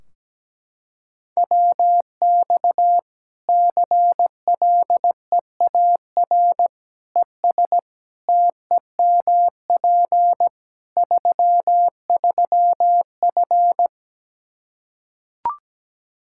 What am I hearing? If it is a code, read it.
Morse code, heard as a keyed tone, about 17 wpm, 700 Hz. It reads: WX CLEAR ES TEMP 33F
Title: Practice QSO - Exchange 2 - Info - 17wpm